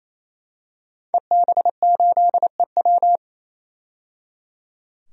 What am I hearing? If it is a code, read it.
Morse code, heard as a keyed tone, about 28 wpm, 700 Hz. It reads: EB8EW
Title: DXCC-100 Call Signs - 5 Characters - 28wpm